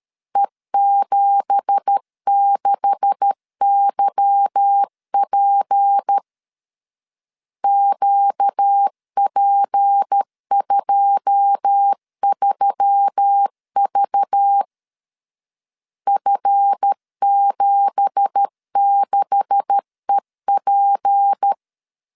Morse code, heard as a keyed tone, about 13 wpm, 800 Hz. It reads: E76YP QP23V F76EP